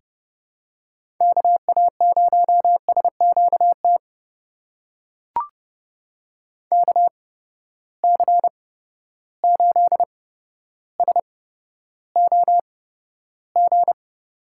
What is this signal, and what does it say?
Morse code, heard as a keyed tone, about 32 wpm, spaced as for 13 wpm, 700 Hz. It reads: KA0SQT KC8SOG
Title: USA Call Signs - 6 Characters - 8x Character Spacing - 30wpm